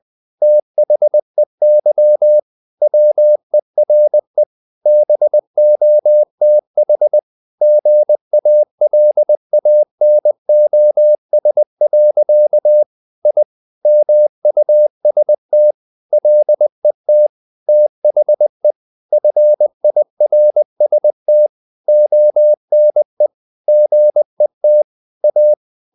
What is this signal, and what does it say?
Morse code, heard as a keyed tone, about 20 wpm, 600 Hz. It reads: THEY WERE BOTH GALANOS. I MUST LET THE FIRST ONE GET A
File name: the old man and the sea part 2 20wpm